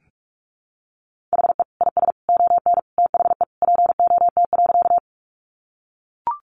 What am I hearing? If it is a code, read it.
Morse code, heard as a keyed tone, about 44 wpm, 700 Hz. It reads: HE IS ON THE POT.
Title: Sentences from Top 1000 Words 45wpm